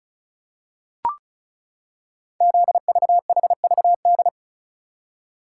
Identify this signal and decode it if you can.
Morse code, heard as a keyed tone, about 36 wpm, 700 Hz. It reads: ZVHVD
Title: Code Groups - 5 Letters - 35wpm